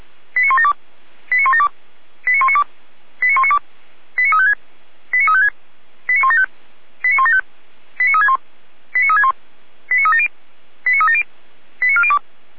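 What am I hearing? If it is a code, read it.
Five-tone selective call (ZVEI), heard as quick runs of five tones, tones of about 80 ms. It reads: 89172 89172 89192 89192 89356 89356 89175 89175 89361 89361 89370 89370 89402
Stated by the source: Five-tone ZVEI selective call received via SDR, from fire brigade Feuerwehr Orlamünde's audio server.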